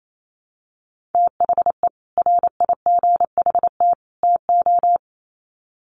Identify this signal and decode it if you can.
Morse code, heard as a keyed tone, about 28 wpm, 700 Hz. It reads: THE RIGHT TO
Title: Sets of 3 Words 28wpm